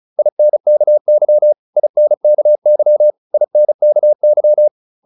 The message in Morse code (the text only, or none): INKY INKY INKY